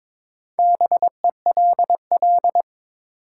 Morse code, heard as a keyed tone, about 23 wpm, 700 Hz. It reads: BELL
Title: Four Letter Words - 22wpm